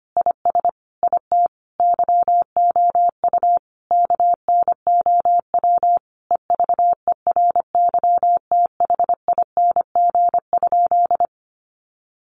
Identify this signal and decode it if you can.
Morse code, heard as a keyed tone, about 25 wpm, 700 Hz. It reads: IS IT YOU KNOW EVERYTHING?
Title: Sentences from Top 700 Words 25wpm